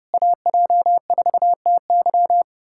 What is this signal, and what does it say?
Morse code, heard as a keyed tone, about 33 wpm, 700 Hz. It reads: AJ4TY